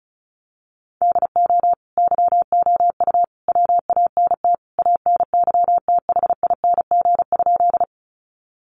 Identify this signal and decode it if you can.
Morse code, heard as a keyed tone, about 34 wpm, 700 Hz. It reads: DO YOU WANT ANYTHING?
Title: Sentences from Top 1000 Words 35wpm